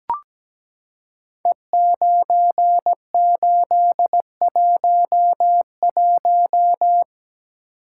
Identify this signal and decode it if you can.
Morse code, heard as a keyed tone, about 17 wpm, 700 Hz. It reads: E9811
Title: Code Groups - 5 Letter:Numbers - 17wpm